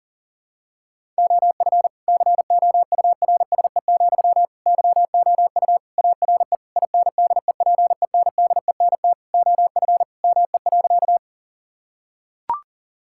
Morse code, heard as a keyed tone, about 40 wpm, 700 Hz. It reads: OF COURSE, YOU ARE INDEPENDENT OF ME.
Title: Sentences from Top 1000 Words - Review Part 6 - 40wpm